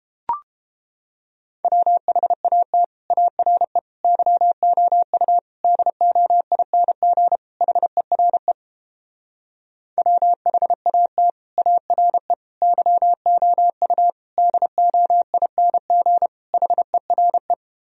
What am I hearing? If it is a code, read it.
Morse code, heard as a keyed tone, about 29 wpm, 700 Hz. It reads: WHAT ARE YOU DOING HERE WHAT ARE YOU DOING HERE